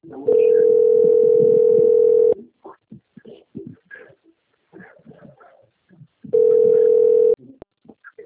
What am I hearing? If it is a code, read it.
Ringback tone (North America), 440 + 480 Hz.